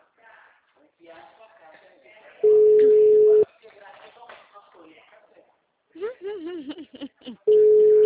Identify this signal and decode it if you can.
Ringback tone (Europe, 425 Hz), with speech.
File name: out- -2015- - - .7968